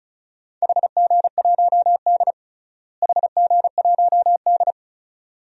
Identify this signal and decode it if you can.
Morse code, heard as a keyed tone, about 36 wpm, 700 Hz. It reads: HG1D HG1D
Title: DXCC-100 Call Signs - 4 Characters - Repeated 2x - 35wpm